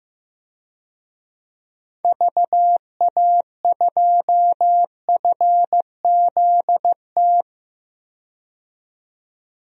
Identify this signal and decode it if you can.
Morse code, heard as a keyed tone, about 15 wpm, 700 Hz. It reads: VA2FZT
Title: DXCC-100 Call Signs - 6 Characters - 15wpm